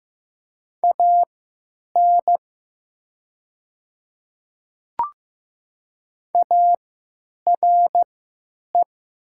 Morse code, heard as a keyed tone, about 15 wpm, spaced as for 9 wpm, 690 Hz. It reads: AN ARE
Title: 3x Character Spacing - Top 100 Words - 15wpm